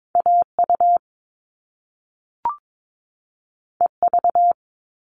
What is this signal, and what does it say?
Morse code, heard as a keyed tone, about 22 wpm, 700 Hz. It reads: AU EV
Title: Top Two Letter Combinations in Words 22wpm